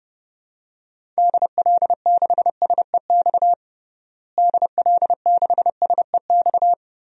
Morse code, heard as a keyed tone, about 32 wpm, 700 Hz. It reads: DL6SEX DL6SEX
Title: DXCC-100 Call Signs - 6 Characters - Repeated 2x - 30wpm